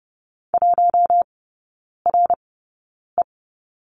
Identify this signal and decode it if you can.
Morse code, heard as a keyed tone, about 28 wpm, spaced as for 9 wpm, 690 Hz. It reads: 1RE
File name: Lesson 16e - Letter c - Three Characters 30wpm